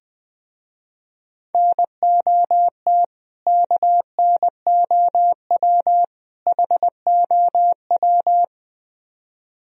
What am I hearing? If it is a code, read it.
Morse code, heard as a keyed tone, about 20 wpm, 700 Hz. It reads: NOT KNOW HOW